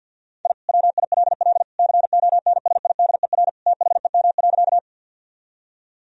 Morse code, heard as a keyed tone, about 50 wpm, 700 Hz. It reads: I WILL CONSIDER THEM.
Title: Sentences from Top 1000 Words - Review Part 4 - 50wpm